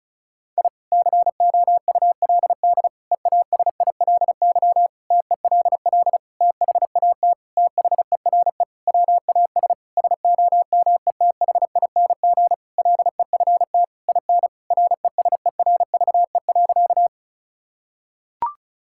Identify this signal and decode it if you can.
Morse code, heard as a keyed tone, about 36 wpm, 700 Hz. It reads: I COULD EASILY TELL THAT THERE WAS SOMETHING LEFT IN RESERVE.
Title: Sentences from Top 1000 Words - Review Part 6 - 35wpm